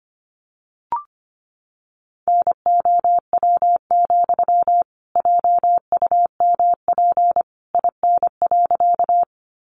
Morse code, heard as a keyed tone, about 25 wpm, 700 Hz. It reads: NOW, JUMP IN.
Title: Sentences from Top 1300 Words 25wpm